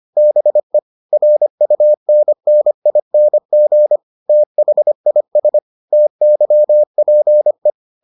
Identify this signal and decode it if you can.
Morse code, heard as a keyed tone, about 25 wpm, 600 Hz. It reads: BE RUNNING THIS TYPE